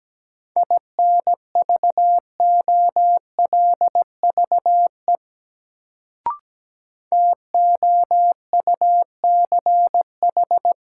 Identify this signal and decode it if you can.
Morse code, heard as a keyed tone, about 17 wpm, 700 Hz. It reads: INVOLVE TOUCH
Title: Sentences from Top 1200 Words - New Word Drill Alt - 17wpm